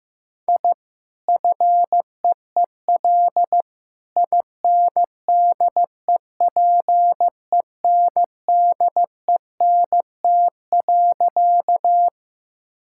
Morse code, heard as a keyed tone, about 15 wpm, 700 Hz. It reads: I FEEL INDEPENDENT.
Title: Sentences from Top 1000 Words - Review Part 6 - 15wpm